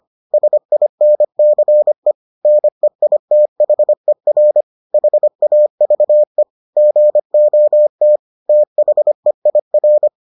Morse code, heard as a keyed tone, about 25 wpm, 600 Hz. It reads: SINCE NEITHER HAVE GOT THEIR